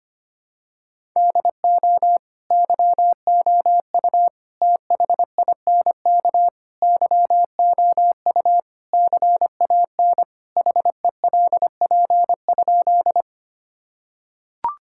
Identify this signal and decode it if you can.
Morse code, heard as a keyed tone, about 25 wpm, 700 Hz. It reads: DO YOU THINK YOU CAN HELP?
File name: practice / 25wpm / Sentences with 4-Word Phrases - 25wpm